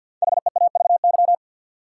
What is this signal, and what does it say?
Morse code, heard as a keyed tone, about 50 wpm, 700 Hz. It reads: HEAVY
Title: Sentences from Top 1100 Words - New Word Drill Alt - 50wpm